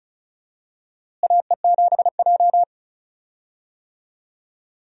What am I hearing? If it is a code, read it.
Morse code, heard as a keyed tone, about 34 wpm, 700 Hz. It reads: AE7J
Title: USA Call Signs - 4 Characters - Repeated 2x - 35wpm